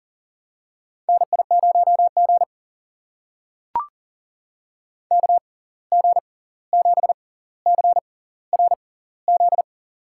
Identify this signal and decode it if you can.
Morse code, heard as a keyed tone, about 40 wpm, spaced as for 18 wpm, 700 Hz. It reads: NI0G KG7CRZ
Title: USA Call Signs - 6x Character Spacing - 40wpm